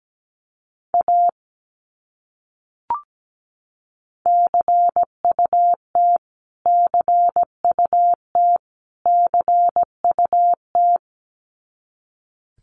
Morse code, heard as a keyed tone, about 17 wpm, 700 Hz. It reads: A CUT CUT CUT
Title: Top 300 Words Repeated 17wpm